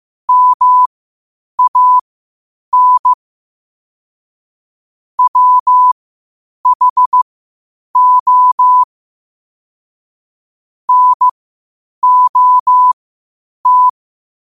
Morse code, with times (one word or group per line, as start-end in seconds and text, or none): 0.3-3.1 MAN
5.2-8.8 WHO
10.9-13.9 NOT